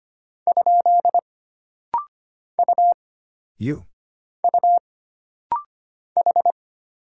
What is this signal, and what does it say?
Morse code, heard as a keyed tone, about 25 wpm, 700 Hz. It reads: ? U U H
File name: Single Character (Letter:Number:Punctuation:Prosign) - Rapid-Fire - 25wpm